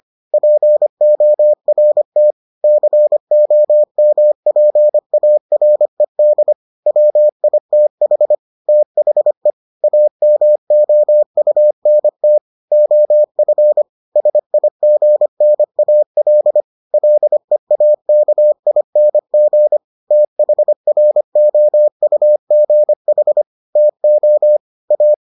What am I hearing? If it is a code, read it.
Morse code, heard as a keyed tone, about 25 wpm, 600 Hz. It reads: PORT COMPARED WITH THE AMOUNT OF SIGNAL LEAKING THROUGH TO A